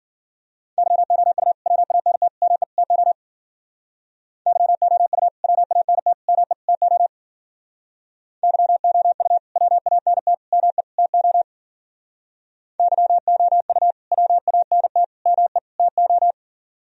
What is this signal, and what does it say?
Morse code, heard as a keyed tone, about 52 wpm, 700 Hz. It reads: YOU WANT ME TO YOU WANT ME TO YOU WANT ME TO YOU WANT ME TO